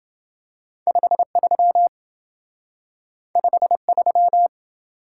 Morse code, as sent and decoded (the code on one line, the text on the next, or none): ..... ...-- / ..... ...--
53 53